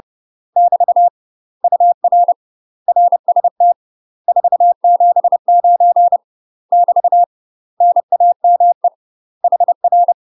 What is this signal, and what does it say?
Morse code, heard as a keyed tone, about 30 wpm, 700 Hz. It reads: <BT> UR RST 479 <BT> NAME HR
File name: UK MASTER QSOs TRAINING / 30WPM / UK MASTER QSOs 0005